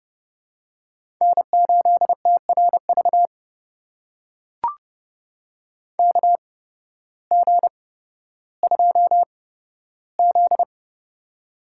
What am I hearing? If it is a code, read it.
Morse code, heard as a keyed tone, about 30 wpm, spaced as for 14 wpm, 700 Hz. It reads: N8TRV KG2Z
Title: USA Call Signs - 8x Character Spacing - 30wpm